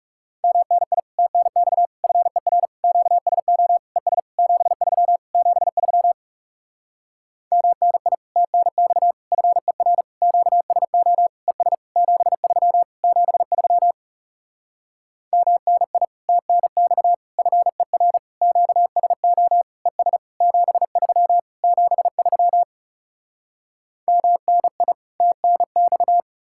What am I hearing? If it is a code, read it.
Morse code, heard as a keyed tone, about 40 wpm, 700 Hz. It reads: MNI TNX FER QSO ES 73 73 MNI TNX FER QSO ES 73 73 MNI TNX FER QSO ES 73 73 MNI TNX